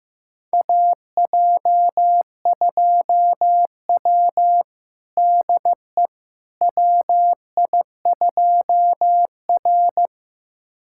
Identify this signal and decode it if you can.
Morse code, heard as a keyed tone, about 15 wpm, 700 Hz. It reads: AJ2W DE WI2R
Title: USA Callsign DE Callsign - 4 Characters - 15wpm